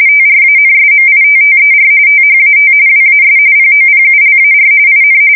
RTTY, about 45.45 baud. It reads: TRANSMITTER THAT WAS POPULAR IN
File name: fsk170 / teletype-RTTY